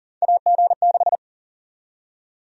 Morse code, heard as a keyed tone, about 40 wpm, 700 Hz. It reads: AG6
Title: USA Call Sign Prefixes - Repeated 2x - 40wpm